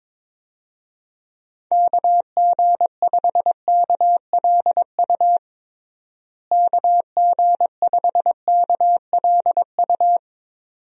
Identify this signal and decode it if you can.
Morse code, heard as a keyed tone, about 22 wpm, 700 Hz. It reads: KG5KLU KG5KLU
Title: USA Call Signs - 6 Characters - Repeated 2x - 22wpm